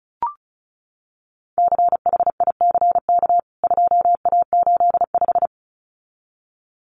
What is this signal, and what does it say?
Morse code, heard as a keyed tone, about 34 wpm, 700 Hz. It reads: CHICK 2A85
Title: CWT Exchange 35wpm